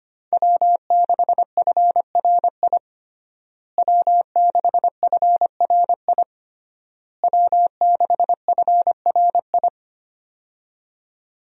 Morse code, heard as a keyed tone, about 25 wpm, 700 Hz. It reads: W6FRI W6FRI W6FRI
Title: USA Call Signs - 5 Characters - Repeated 3x - 25wpm